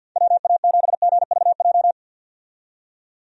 Morse code, heard as a keyed tone, about 46 wpm, 700 Hz. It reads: WA7GVJ
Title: USA Call Signs - 6 Characters - 50wpm